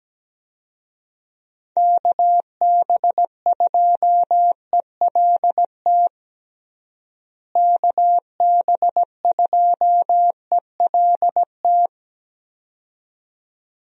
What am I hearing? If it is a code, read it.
Morse code, heard as a keyed tone, about 17 wpm, 700 Hz. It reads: KB2ELT KB2ELT
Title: USA Call Signs - 6 Characters - Repeated 2x - 17wpm